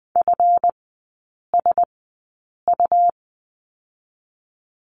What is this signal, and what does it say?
Morse code, heard as a keyed tone, about 20 wpm, spaced as for 8 wpm, 700 Hz. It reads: FSU
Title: 138 Lesson 23e - Letter f - Three Characters 20wpm